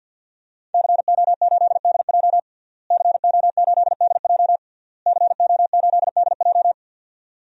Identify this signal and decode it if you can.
Morse code, heard as a keyed tone, about 52 wpm, 700 Hz. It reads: CO8DJ CO8DJ CO8DJ
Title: DXCC-100 Call Signs - 5 Characters - Repeated 3x - 50wpm